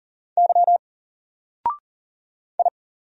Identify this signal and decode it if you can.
Morse code, heard as a keyed tone, about 40 wpm, 700 Hz. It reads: Y I
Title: Single Character (Letter:Number:Punctuation:Prosign) 40wpm